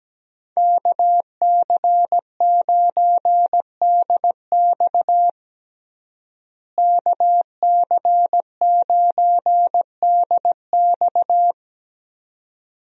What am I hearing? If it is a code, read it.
Morse code, heard as a keyed tone, about 17 wpm, 700 Hz. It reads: KC9DX KC9DX
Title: USA Call Signs - 5 Characters - Repeated 2x - 17wpm